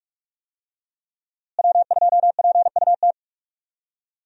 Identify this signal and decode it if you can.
Morse code, heard as a keyed tone, about 46 wpm, 700 Hz. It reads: W2PUT
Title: USA Call Signs - 5 Characters - Repeated 2x - 45wpm